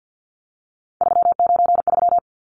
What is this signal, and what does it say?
Morse code, heard as a keyed tone, about 50 wpm, 700 Hz. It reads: 393
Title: Numbers - 3-Digit 50wpm